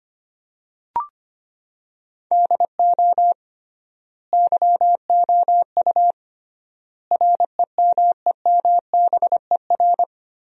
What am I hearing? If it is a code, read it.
Morse code, heard as a keyed tone, about 25 wpm, 700 Hz. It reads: DO YOU REMEMBER
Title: 3x Word-Spacing - Sentences from Top 400 Words - 25wpm